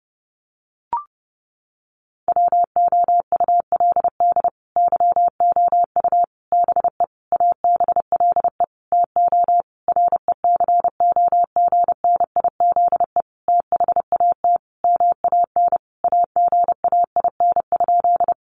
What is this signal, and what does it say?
Morse code, heard as a keyed tone, about 30 wpm, 700 Hz. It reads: WOULD YOU BE ABLE TO RECOGNIZE THAT MAN AGAIN?
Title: Sentences from Top 1400 Words 30wpm